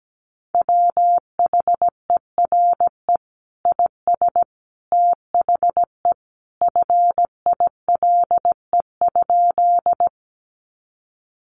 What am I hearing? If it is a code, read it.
Morse code, heard as a keyed tone, about 17 wpm, 700 Hz. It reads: WHERE IS THE FILE?
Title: Sentences from Top 1200 Words 17wpm